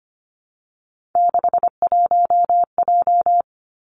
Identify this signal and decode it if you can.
Morse code, heard as a keyed tone, about 25 wpm, 700 Hz. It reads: 61J
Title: Three Challenging Letter-Number Combinations 25wpm